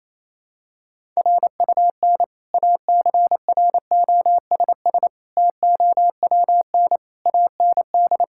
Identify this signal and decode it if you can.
Morse code, heard as a keyed tone, about 27 wpm, 700 Hz. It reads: RUN ACROSS TOWN AND
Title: Sentences from Top 1000 Words - Review Part 5 - 28wpm